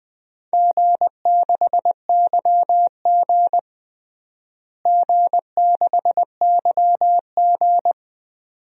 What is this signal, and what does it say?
Morse code, heard as a keyed tone, about 20 wpm, 700 Hz. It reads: G6YG G6YG
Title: DXCC-100 Call Signs - 4 Characters - Repeated 2x - 20wpm